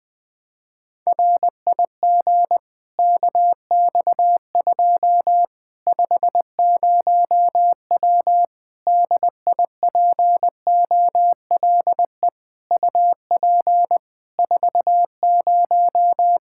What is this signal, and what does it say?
Morse code, heard as a keyed tone, about 20 wpm, 700 Hz. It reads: RIG KX2 50W DIPOLE UP 40
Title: Practice QSO - Exchange 2 - Info - 20wpm